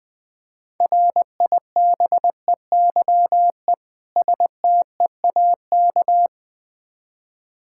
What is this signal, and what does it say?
Morse code, heard as a keyed tone, about 20 wpm, 700 Hz. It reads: RIBEYE STEAK
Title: Difficult Words with Hints - 20wpm